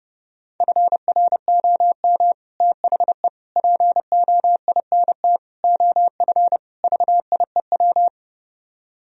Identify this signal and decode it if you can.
Morse code, heard as a keyed tone, about 32 wpm, 700 Hz. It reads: FROM THE POINT OF VIEW